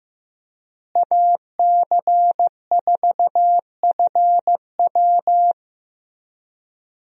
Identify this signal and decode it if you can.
Morse code, heard as a keyed tone, about 15 wpm, 700 Hz. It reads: AC4FW
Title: USA Call Signs - 5 Characters - Repeated 3x - 15wpm